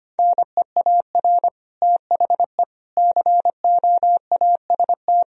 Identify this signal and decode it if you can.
Morse code, heard as a keyed tone, about 25 wpm, 700 Hz. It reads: NEAR THE COAST
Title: Sentences from Top 1000 Words - Review Part 4 - 25wpm